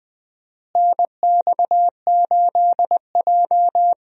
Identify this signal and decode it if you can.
Morse code, heard as a keyed tone, about 20 wpm, 700 Hz. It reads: NX8J
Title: USA Call Signs - 4 Characters - Repeated 3x - 20wpm